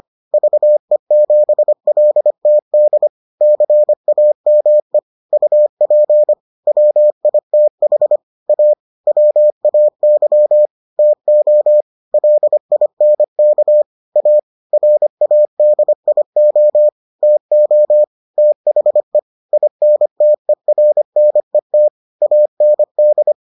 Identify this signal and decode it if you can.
Morse code, heard as a keyed tone, about 25 wpm, 600 Hz. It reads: VE7LTD CAME UP WITH A WAY TO LINK A RADIO TO THE INTERNET AND